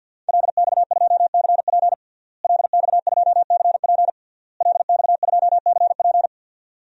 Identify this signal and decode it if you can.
Morse code, heard as a keyed tone, about 50 wpm, 700 Hz. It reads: LX2CP LX2CP LX2CP
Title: DXCC-100 Call Signs - 5 Characters - Repeated 3x - 50wpm